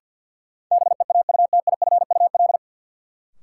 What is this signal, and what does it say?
Morse code, heard as a keyed tone, about 52 wpm, 700 Hz. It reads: BEAUTIFUL
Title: Sentences from Top 900 Words - New Word Drill - 50wpm